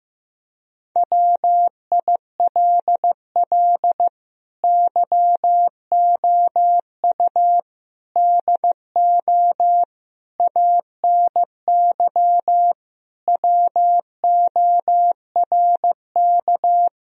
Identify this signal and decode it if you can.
Morse code, heard as a keyed tone, about 15 wpm, 700 Hz. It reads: WILL YOU DO ANY WORK
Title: Sentences from Top 1000 Words - Review Part 3 - 15wpm